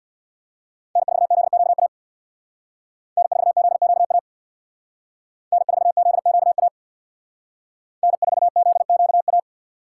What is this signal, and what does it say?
Morse code, heard as a keyed tone, about 63 wpm, 700 Hz. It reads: N4ZQA N4ZQA N4ZQA N4ZQA